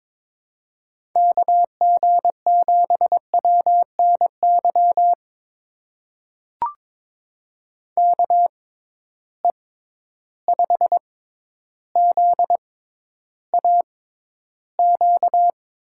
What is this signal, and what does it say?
Morse code, heard as a keyed tone, about 22 wpm, spaced as for 11 wpm, 700 Hz. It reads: KG7WNY KE5ZAQ
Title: USA Call Signs - 6 Characters - 6x Character Spacing - 22wpm